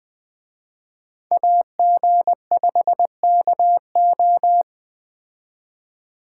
Morse code, heard as a keyed tone, about 20 wpm, 700 Hz. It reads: AG5KO
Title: Speed Racing - USA 2x2 Call Signs 17wpm